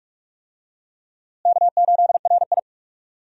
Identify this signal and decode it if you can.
Morse code, heard as a keyed tone, about 46 wpm, 700 Hz. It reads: K8RI
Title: USA Call Signs - 4 Characters - Repeated 3x - 45wpm